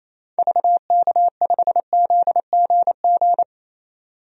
Morse code, heard as a keyed tone, about 28 wpm, 700 Hz. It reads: VK5ZGG
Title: DXCC-100 Call Signs - 6 Characters - 28wpm